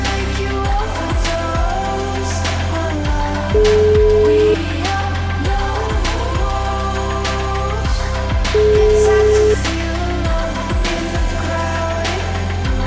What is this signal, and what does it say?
Ringback tone (Europe, 425 Hz), with noise.